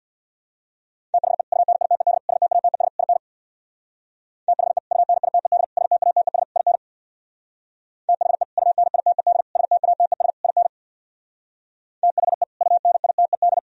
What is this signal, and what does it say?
Morse code, heard as a keyed tone, about 62 wpm, 700 Hz. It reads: THE UNITED STATES IN THE UNITED STATES IN THE UNITED STATES IN THE UNITED
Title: Speed-Racing - Sets of 4 Words 50wpm